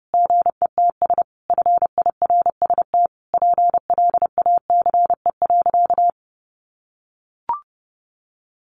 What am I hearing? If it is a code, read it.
Morse code, heard as a keyed tone, about 30 wpm, 700 Hz. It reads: GETS FIRST PLACE.
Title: Additional Easy Phrases 30wpm